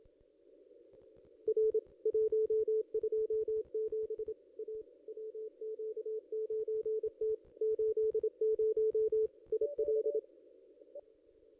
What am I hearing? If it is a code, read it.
Morse code, heard as a keyed tone, about 27 wpm, 440 Hz. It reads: R 127 A WQ9T 80 IL
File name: contests / ss-cw-k5tr-op-n5rz -1900